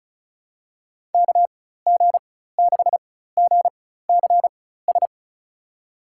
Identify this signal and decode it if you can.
Morse code, heard as a keyed tone, about 36 wpm, spaced as for 16 wpm, 700 Hz. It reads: KG6GCS
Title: USA Call Signs - 6 Characters - 4x Character Spacing - 35wpm